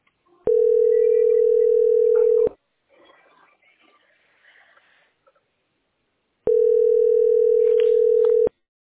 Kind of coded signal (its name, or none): telephone call progress tone